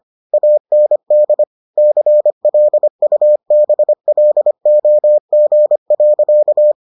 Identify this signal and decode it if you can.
Morse code, heard as a keyed tone, about 25 wpm, 600 Hz. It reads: AND CLUBLOG.